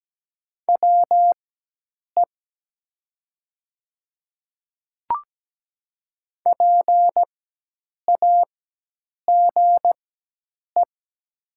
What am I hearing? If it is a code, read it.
Morse code, heard as a keyed tone, about 17 wpm, spaced as for 8 wpm, 700 Hz. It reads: WE PAGE
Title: 4x Character Spacing - Top 500 Words - 17wpm